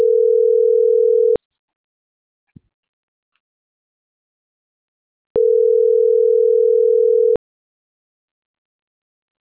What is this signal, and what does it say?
Ringback tone (North America), 440 + 480 Hz.